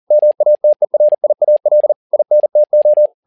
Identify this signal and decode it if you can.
Morse code, heard as a keyed tone, about 40 wpm, 600 Hz. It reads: MATERIAL INTO
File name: learning center / tower trouble / tower touble 40wpm